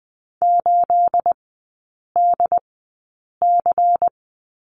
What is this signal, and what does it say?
Morse code, heard as a keyed tone, about 20 wpm, spaced as for 8 wpm, 700 Hz. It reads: 8DC
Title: 212 Lesson 34e - Number 8 - Three Characters 20wpm